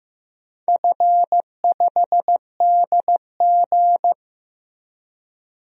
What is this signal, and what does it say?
Morse code, heard as a keyed tone, about 16 wpm, 700 Hz. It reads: F5DG
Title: DXCC-100 Call Signs - 4 Characters - 15wpm